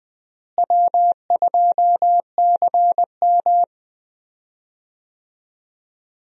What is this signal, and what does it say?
Morse code, heard as a keyed tone, about 20 wpm, 700 Hz. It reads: W2CM
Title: USA Call Signs - 4 Characters - Repeated 2x - 20wpm